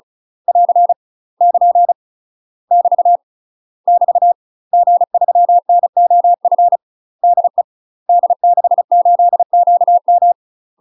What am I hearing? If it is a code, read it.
Morse code, heard as a keyed tone, about 36 wpm, 700 Hz. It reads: <AR> <KN> <BT> <BT> G3NOF DE D68QM
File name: UK MASTER QSOs TRAINING / 35WPM / UK MASTER QSOs 0020